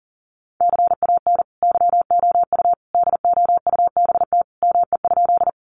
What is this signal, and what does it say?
Morse code, heard as a keyed tone, about 40 wpm, 700 Hz. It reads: CAN YOU DOUBT ME?
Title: Sentences from Top 1500 Words 40wpm